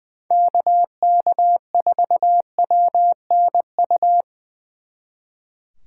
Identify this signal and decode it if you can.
Morse code, heard as a keyed tone, about 21 wpm, 700 Hz. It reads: KK4WNU
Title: Speed Racing - USA 2x3 Call Signs 20wpm